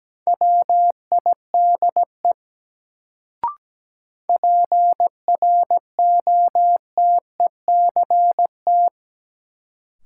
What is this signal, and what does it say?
Morse code, heard as a keyed tone, about 17 wpm, 700 Hz. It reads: WIDE PROTECT
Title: Sentences from Top 900 Words - New Word Drill - 17wpm